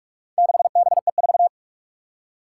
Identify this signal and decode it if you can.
Morse code, heard as a keyed tone, about 46 wpm, 700 Hz. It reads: 6BE4
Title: Four Challenging Letter-Number Combinations 45wpm